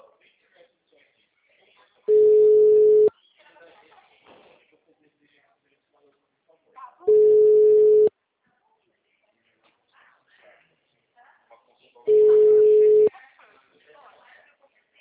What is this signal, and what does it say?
Ringback tone (Europe, 425 Hz), with speech.